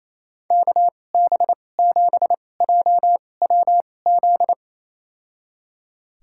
Morse code, heard as a keyed tone, about 28 wpm, spaced as for 20 wpm, 700 Hz. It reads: KB7JWZ
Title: USA Call Signs - 6 Characters - 2x Character Spacing - 28wpm